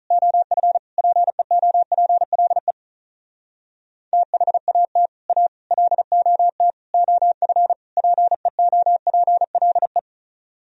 Morse code, heard as a keyed tone, about 45 wpm, 700 Hz. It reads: OF PEOPLE THAT A LOT OF PEOPLE